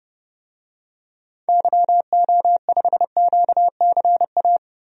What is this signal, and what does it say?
Morse code, heard as a keyed tone, about 30 wpm, 700 Hz. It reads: YO5QCA
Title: DXCC-100 Call Signs - 6 Characters - 30wpm